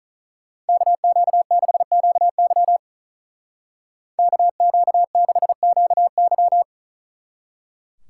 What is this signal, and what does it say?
Morse code, heard as a keyed tone, about 45 wpm, 700 Hz. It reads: KQ6QY KQ6QY